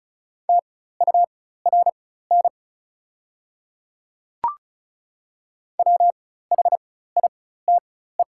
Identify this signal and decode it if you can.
Morse code, heard as a keyed tone, about 36 wpm, spaced as for 16 wpm, 690 Hz. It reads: TURN WHITE
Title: 4x Character Spacing - Top 500 Words - 35wpm